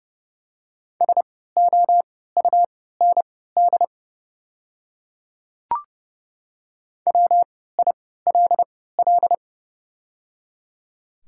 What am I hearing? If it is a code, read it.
Morse code, heard as a keyed tone, about 30 wpm, spaced as for 17 wpm, 700 Hz. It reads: SOUND WILL
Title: 3x Character Spacing - Top 100 Words - 30wpm